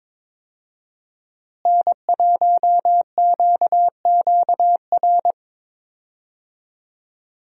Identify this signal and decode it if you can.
Morse code, heard as a keyed tone, about 22 wpm, 700 Hz. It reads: N1QQR
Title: USA Call Signs - 5 Characters - Repeated 2x - 22wpm